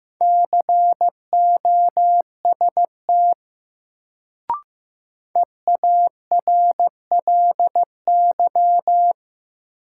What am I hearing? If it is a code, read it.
Morse code, heard as a keyed tone, about 15 wpm, 700 Hz. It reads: COST EARLY
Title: Sets of 3 Words - New Word Drill - 15wpm